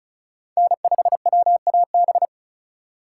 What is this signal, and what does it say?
Morse code, heard as a keyed tone, about 36 wpm, 700 Hz. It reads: N5WAB
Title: USA Call Signs - 3x Character Spacing - 35wpm